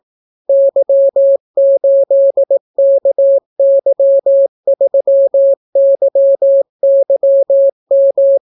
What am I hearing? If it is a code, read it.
Morse code, heard as a keyed tone, about 18 wpm, 550 Hz. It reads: Y8KY3YYM